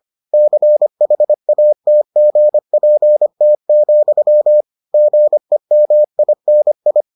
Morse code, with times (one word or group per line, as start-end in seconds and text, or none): 0.3-4.6 CHATGPT,
4.9-7.0 GEMINI